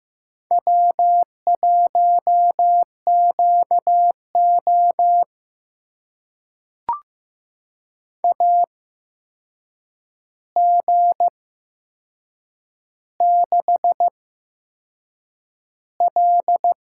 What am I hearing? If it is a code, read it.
Morse code, heard as a keyed tone, about 15 wpm, spaced as for 6 wpm, 700 Hz. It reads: W1QO AG6L
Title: USA Call Signs - 4 Characters - 8x Character Spacing - 15wpm